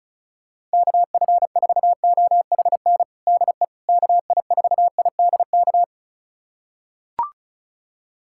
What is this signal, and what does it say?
Morse code, heard as a keyed tone, about 36 wpm, 700 Hz. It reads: KF4OHN DE KI4IDK
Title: USA Callsign DE Callsign - 6 Characters - 35wpm